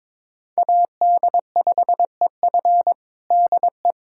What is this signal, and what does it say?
Morse code, heard as a keyed tone, about 22 wpm, 700 Hz. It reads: AD5EF DE
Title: USA Callsign DE Callsign - 5 Characters - 22wpm